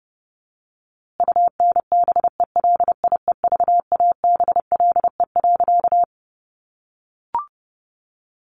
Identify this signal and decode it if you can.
Morse code, heard as a keyed tone, about 30 wpm, 700 Hz. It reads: UNBELIEVABLE.
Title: A Sound Of Thunder 30wpm